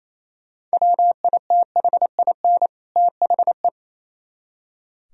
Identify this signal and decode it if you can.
Morse code, heard as a keyed tone, about 33 wpm, 700 Hz. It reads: WITHIN THE
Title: Speed-Racing - Sets of 2 Words 28wpm